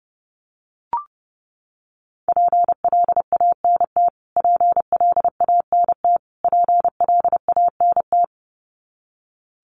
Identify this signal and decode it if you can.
Morse code, heard as a keyed tone, about 30 wpm, 700 Hz. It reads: PLANT PLANT PLANT
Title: Top 500 Words Repeated 30wpm